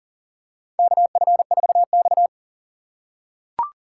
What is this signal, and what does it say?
Morse code, heard as a keyed tone, about 40 wpm, 700 Hz. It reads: KF4X
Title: USA Call Signs - 4 Characters - 4x Character Spacing - 40wpm